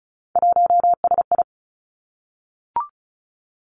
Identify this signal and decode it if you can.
Morse code, heard as a keyed tone, about 36 wpm, 700 Hz. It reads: 1SI
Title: Three Challenging Letter-Number Combinations 35wpm